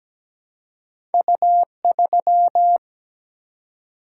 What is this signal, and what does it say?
Morse code, heard as a keyed tone, about 17 wpm, 700 Hz. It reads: U3
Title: Single-Letter Single-Number 17wpm